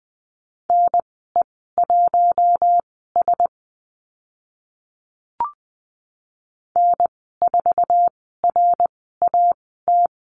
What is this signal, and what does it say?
Morse code, heard as a keyed tone, about 20 wpm, spaced as for 14 wpm, 700 Hz. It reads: NE1S N4RAT
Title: 055 Lesson 10g - Letter r - Call Signs 20wpm